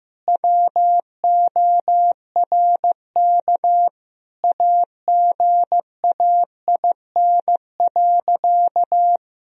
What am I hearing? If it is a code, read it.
Morse code, heard as a keyed tone, about 15 wpm, 700 Hz. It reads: WORK AGAIN.
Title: Sentences from Top 1000 Words - Review Part 2 - 15wpm